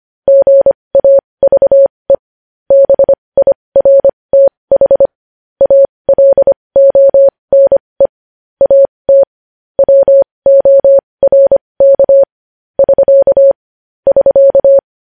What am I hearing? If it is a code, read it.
Morse code, heard as a keyed tone, about 25 wpm, spaced as for 20 wpm, 550 Hz. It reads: GAVE BIRTH ALONE AT WORK <SK> <SK>
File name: podcast / News headlines-1x-25x20